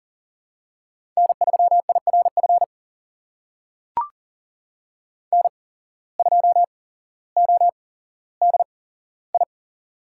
Morse code, heard as a keyed tone, about 37 wpm, spaced as for 17 wpm, 700 Hz. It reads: N3IRF N2ODI
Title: USA Call Signs - 5 Characters - 8x Character Spacing - 40wpm